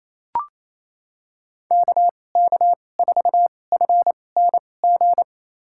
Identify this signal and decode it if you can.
Morse code, heard as a keyed tone, about 27 wpm, spaced as for 20 wpm, 700 Hz. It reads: KK4FNG
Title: USA Call Signs - 6 Characters - 2x Character Spacing - 28wpm